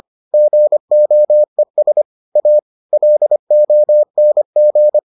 Morse code, heard as a keyed tone, about 25 wpm, 600 Hz. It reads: GOES A LONG